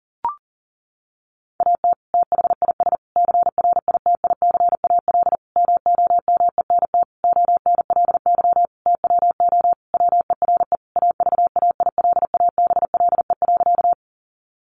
Morse code, heard as a keyed tone, about 40 wpm, 700 Hz. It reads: AT THIS CRITICAL MOMENT ONLY TWO WERE AVAILABLE.
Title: Sentences from Top 1100 Words 40wpm